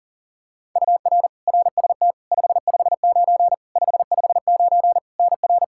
Morse code, heard as a keyed tone, about 40 wpm, 700 Hz. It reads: UR RST 559 559 NR
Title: Practice QSO - Exchange 1 - Beginning - 40wpm